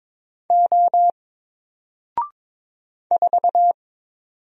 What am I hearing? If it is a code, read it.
Morse code, heard as a keyed tone, about 22 wpm, 700 Hz. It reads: O 4
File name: Single-Letter-Number-22wpm